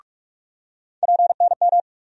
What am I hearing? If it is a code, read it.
Morse code, heard as a keyed tone, about 46 wpm, 700 Hz. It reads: PNM
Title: USA Call Sign Postfixes - 3 Letters 45wpm